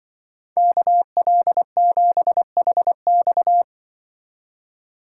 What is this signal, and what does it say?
Morse code, heard as a keyed tone, about 24 wpm, 700 Hz. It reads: KL7HX
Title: Speed Racing - USA 2x2 Call Signs 20wpm